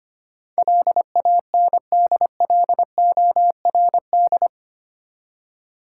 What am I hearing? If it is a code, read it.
Morse code, heard as a keyed tone, about 25 wpm, 700 Hz. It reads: LANDLORD
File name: Eight Letter Words - 25wpm